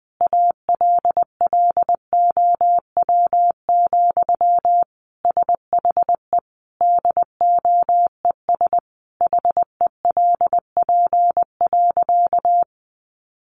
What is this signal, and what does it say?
Morse code, heard as a keyed tone, about 20 wpm, 700 Hz. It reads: ALLOW, SHE DOES HELP.
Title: Sentences from Top 700 Words 20wpm